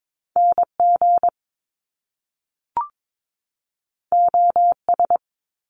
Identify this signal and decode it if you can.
Morse code, heard as a keyed tone, about 22 wpm, 700 Hz. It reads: NG OS
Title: Top Two Letter Combinations in Words 22wpm